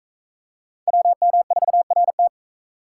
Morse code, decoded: WM4RT

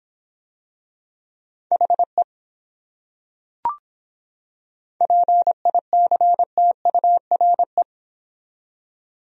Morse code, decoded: HE PICTURE

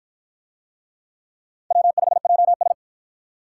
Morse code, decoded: W5JS

